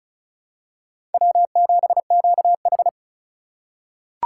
Morse code, heard as a keyed tone, about 36 wpm, 690 Hz. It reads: W7QH
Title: USA Call Signs - 3x Character Spacing - 35wpm